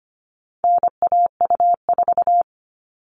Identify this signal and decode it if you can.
Morse code, heard as a keyed tone, about 25 wpm, 700 Hz. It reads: NAU4
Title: All Cut Numbers - 4-Digit 25wpm